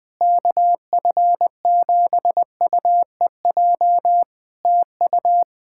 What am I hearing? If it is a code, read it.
Morse code, heard as a keyed tone, about 21 wpm, 700 Hz. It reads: KF7UEJ TU
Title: K1USN SST QSO Practice 20wpm